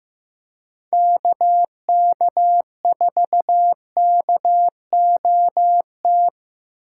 Morse code, heard as a keyed tone, about 16 wpm, 700 Hz. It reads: KK4KOT
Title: USA Call Signs - 6 Characters - 8x Character Spacing - 15wpm